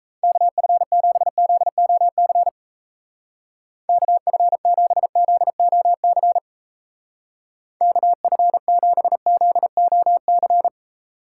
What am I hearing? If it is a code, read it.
Morse code, heard as a keyed tone, about 35 wpm, 700 Hz. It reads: KF7ZOC KF7ZOC KF7ZOC
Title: Speed Racing - USA 2x3 Call Signs 28wpm